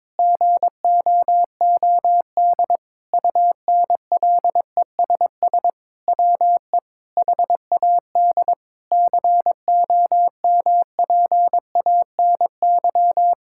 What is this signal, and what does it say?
Morse code, heard as a keyed tone, about 22 wpm, 700 Hz. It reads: GOOD UNLESS WE HAD COMPANY
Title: Sentences from Top 1000 Words - Review Part 6 - 22wpm